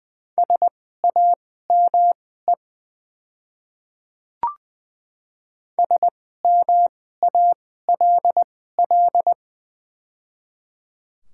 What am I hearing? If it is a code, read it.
Morse code, heard as a keyed tone, about 21 wpm, spaced as for 15 wpm, 700 Hz. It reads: SAME SMALL
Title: 2x Character Spacing - Top 200 Words - 20wpm